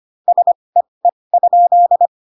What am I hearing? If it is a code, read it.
Morse code, heard as a keyed tone, about 25 wpm, spaced as for 20 wpm, 700 Hz. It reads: SEE?
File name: Pinocchio / Pinocchio ch3 15 25 ik1hge